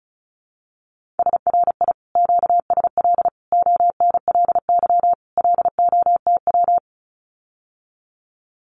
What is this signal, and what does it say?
Morse code, heard as a keyed tone, about 34 wpm, 700 Hz. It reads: SRI QSL ONLY LOTW
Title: Common QSO Phrases Encore 35wpm